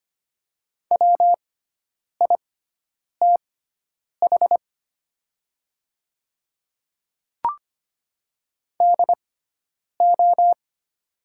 Morse code, heard as a keyed tone, about 25 wpm, spaced as for 9 wpm, 700 Hz. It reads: WITH DO
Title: 6x Character Spacing - Top 100 Words - 25wpm